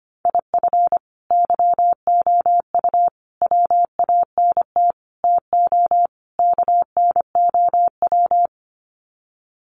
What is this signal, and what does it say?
Morse code, heard as a keyed tone, about 25 wpm, 700 Hz. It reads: IF YOU WANT TO KNOW
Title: Sets of 5 Words Encore2 25wpm